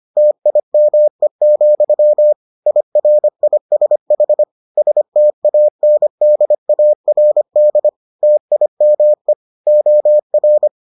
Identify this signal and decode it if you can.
Morse code, heard as a keyed tone, about 25 wpm, 600 Hz. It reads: TIME, IRISH STANDARD TIME OR